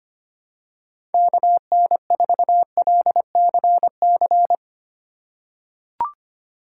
Morse code, heard as a keyed tone, about 25 wpm, 700 Hz. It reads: KN4LCC
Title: USA Call Signs - 6 Characters - 4x Character Spacing - 25wpm